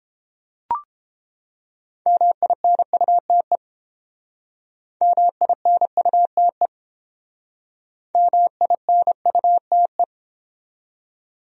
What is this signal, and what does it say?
Morse code, heard as a keyed tone, about 29 wpm, 700 Hz. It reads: MINUTE MINUTE MINUTE